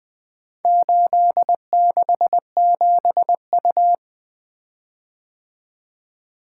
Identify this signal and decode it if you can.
Morse code, heard as a keyed tone, about 21 wpm, 700 Hz. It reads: 867U
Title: Four Letter-Number Combinations - 20wpm